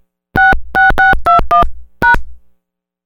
DTMF, heard as telephone touch tones, tones of about 150 ms. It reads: 66621#